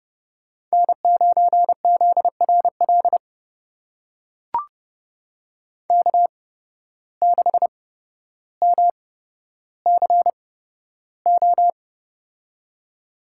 Morse code, heard as a keyed tone, about 31 wpm, spaced as for 13 wpm, 700 Hz. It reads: N9ZRL K6MCO
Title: USA Call Signs - 5 Characters - 8x Character Spacing - 30wpm